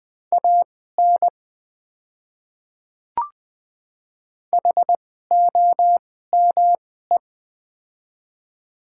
Morse code, heard as a keyed tone, about 20 wpm, spaced as for 14 wpm, 700 Hz. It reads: AN HOME
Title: 2x Character Spacing - Top 300 Words - 20wpm